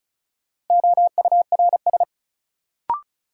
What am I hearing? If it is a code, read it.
Morse code, heard as a keyed tone, about 34 wpm, 700 Hz. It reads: OURS
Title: Sentences from Top 1300 Words - New Word Drill Alt - 35wpm